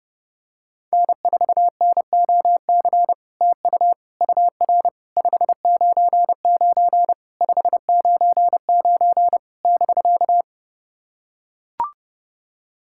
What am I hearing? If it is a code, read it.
Morse code, heard as a keyed tone, about 30 wpm, 700 Hz. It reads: N4NOC TU UR 599 599 <BK>
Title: SOTA QSO Practice 30wpm